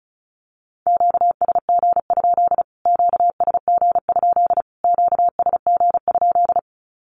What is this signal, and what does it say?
Morse code, heard as a keyed tone, about 34 wpm, 700 Hz. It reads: QSG? QSG? QSG?
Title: Q-Codes Repeated 3x 35wpm